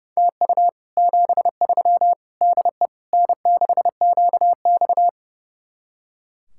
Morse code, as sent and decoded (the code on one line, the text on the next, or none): - ..- / --... ...-- / -.. / -. -.... --.- -..-
TU 73 DE N6QX